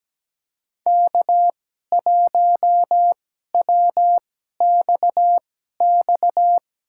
Morse code, heard as a keyed tone, about 17 wpm, spaced as for 12 wpm, 700 Hz. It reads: K1WXX
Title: USA Call Signs - 5 Characters - 2x Character Spacing - 17wpm